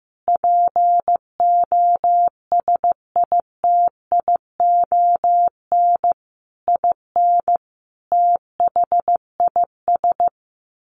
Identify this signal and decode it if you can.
Morse code, heard as a keyed tone, about 15 wpm, 700 Hz. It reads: POSITION IN THIS
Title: Sentences from Top 1200 Words 15wpm